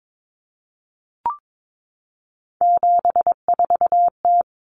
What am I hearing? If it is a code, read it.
Morse code, heard as a keyed tone, about 22 wpm, 700 Hz. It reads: 74T
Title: Common Cut Numbers - 3-Digit 22wpm